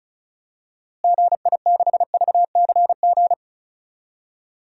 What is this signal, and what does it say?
Morse code, heard as a keyed tone, about 36 wpm, 700 Hz. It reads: GI6VCG
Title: DXCC-100 Call Signs - 6 Characters - Repeated 2x - 35wpm